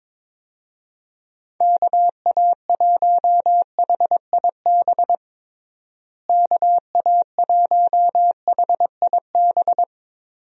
Morse code, heard as a keyed tone, about 21 wpm, 700 Hz. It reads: KA1HIB KA1HIB
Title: USA Call Signs - 6 Characters - Repeated 2x - 22wpm